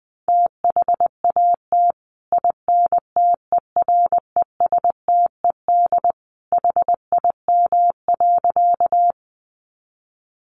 Morse code, heard as a keyed tone, about 20 wpm, 700 Hz. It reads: THAT INTERESTED HIM.
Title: Sentences from Top 900 Words 20wpm